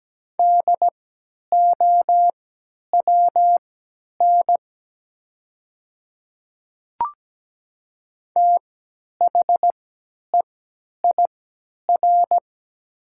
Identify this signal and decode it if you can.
Morse code, heard as a keyed tone, about 17 wpm, spaced as for 10 wpm, 700 Hz. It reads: DOWN THEIR
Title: 3x Character Spacing - Top 300 Words - 17wpm